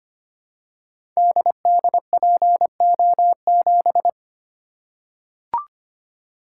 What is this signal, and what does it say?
Morse code, heard as a keyed tone, about 25 wpm, 700 Hz. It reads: DDPO7
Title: Code Groups - 5 Letter:Numbers - 25wpm